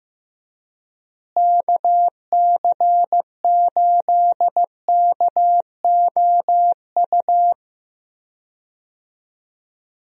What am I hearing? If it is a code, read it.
Morse code, heard as a keyed tone, about 15 wpm, 700 Hz. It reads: KC8KOU
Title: USA Call Signs - 6 Characters - Repeated 3x - 15wpm